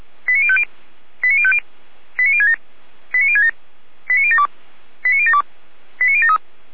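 Five-tone selective call (ZVEI), heard as quick runs of five tones, tones of about 70 ms. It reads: 89050 89050 89067 89067 89072 89072 89073
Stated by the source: ZVEI five-tone selective call received on SDR.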